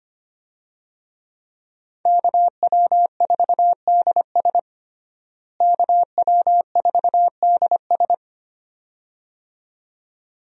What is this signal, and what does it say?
Morse code, heard as a keyed tone, about 25 wpm, 700 Hz. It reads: KW4DS KW4DS
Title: USA Call Signs - 5 Characters - Repeated 2x - 25wpm